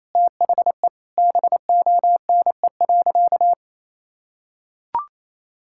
Morse code, heard as a keyed tone, about 28 wpm, 700 Hz. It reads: THE BONE.
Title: Sentences from Top 1600 Words 28wpm